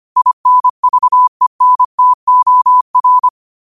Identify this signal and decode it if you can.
Morse code, heard as a keyed tone, about 25 wpm, 1000 Hz. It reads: INVENTOR
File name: qotdfiles / 25wpm